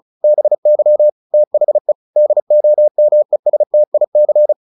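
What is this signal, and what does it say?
Morse code, heard as a keyed tone, about 34 wpm, 600 Hz. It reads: BY THE DOMESTIC
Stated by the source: Morse practice at 35 wpm from CW med Gnister, sending Ken Follett's novel Never in English.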